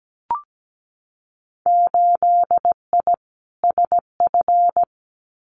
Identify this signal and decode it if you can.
Morse code, heard as a keyed tone, about 17 wpm, 700 Hz. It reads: 8I SF
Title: Winter Field Day Exchanges Encore 17wpm